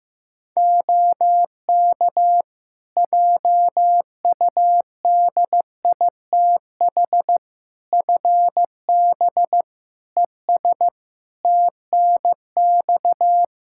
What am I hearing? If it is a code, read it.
Morse code, heard as a keyed tone, about 15 wpm, 700 Hz. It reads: OK JUDITH FB ES TNX
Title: Practice QSO - Exchange 2 - Info - 15wpm